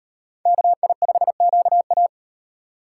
Morse code, heard as a keyed tone, about 36 wpm, 700 Hz. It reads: KI5QA